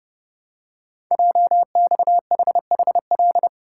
Morse code, heard as a keyed tone, about 30 wpm, 700 Hz. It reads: JXHHL
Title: Code Groups - 5 Letters - 30wpm